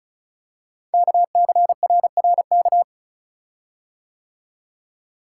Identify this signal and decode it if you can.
Morse code, heard as a keyed tone, about 36 wpm, 690 Hz. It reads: KCRRK
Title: Code Groups - 5 Letter:Numbers - 35wpm